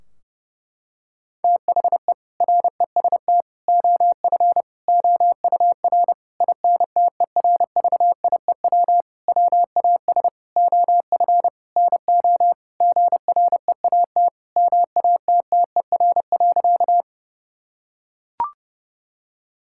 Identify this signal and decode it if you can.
Morse code, heard as a keyed tone, about 29 wpm, 700 Hz. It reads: THE REST OF OUR INTERVIEW WAS OF NO GREAT MATTER.
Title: Sentences from Top 1600 Words 30wpm